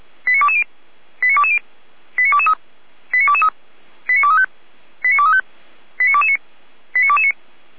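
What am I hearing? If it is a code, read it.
Five-tone selective call (ZVEI), heard as quick runs of five tones, tones of about 70 ms. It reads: 89220 89220 89223 89223 89235 89235 89229 89229
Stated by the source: ZVEI five-tone call received with an SDR.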